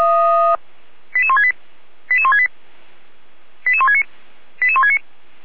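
Five-tone selective call (ZVEI), heard as quick runs of five tones, tones of about 70 ms. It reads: 88168 88168 88169 88169
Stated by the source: ZVEI five-tone selective call received by SDR.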